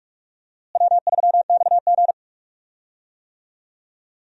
Morse code, heard as a keyed tone, about 46 wpm, 700 Hz. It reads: W3XG
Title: USA Call Signs - 4 Characters - Repeated 3x - 45wpm